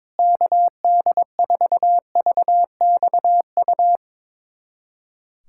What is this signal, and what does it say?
Morse code, heard as a keyed tone, about 22 wpm, 700 Hz. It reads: KD4VXU
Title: USA Callsign DE Callsign - 6 Characters - 22wpm